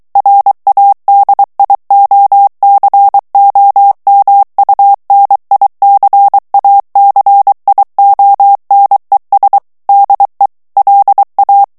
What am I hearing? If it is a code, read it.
Morse code, heard as a keyed tone, about 23 wpm, 800 Hz. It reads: RADIOCOMUNICACIONES DE LA
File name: morse mp3 / ESPAGNOL TEXTE EN CLAIR 2 VITESSE 22 MOTS MINUTES